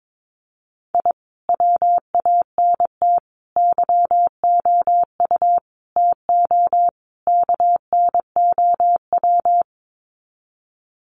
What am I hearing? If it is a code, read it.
Morse code, heard as a keyed tone, about 22 wpm, 700 Hz. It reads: I WANT YOU TO KNOW
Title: Sets of 5 Words Encore 22wpm